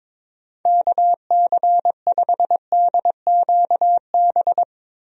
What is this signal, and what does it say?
Morse code, heard as a keyed tone, about 22 wpm, 700 Hz. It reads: KC5DQB
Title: USA Call Signs - 6 Characters - Repeated 3x - 22wpm